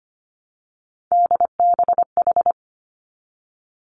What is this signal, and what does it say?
Morse code, heard as a keyed tone, about 25 wpm, 700 Hz. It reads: DBH
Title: Three Challenging Letter-Number Combinations 25wpm